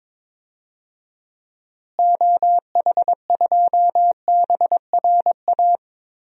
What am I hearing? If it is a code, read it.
Morse code, heard as a keyed tone, about 22 wpm, 700 Hz. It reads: OH2BRA
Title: DXCC-100 Call Signs - 6 Characters - Repeated 2x - 22wpm